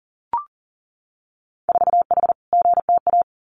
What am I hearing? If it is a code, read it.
Morse code, heard as a keyed tone, about 41 wpm, 700 Hz. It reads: 4H GTA
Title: Winter Field Day Exchanges Encore 40wpm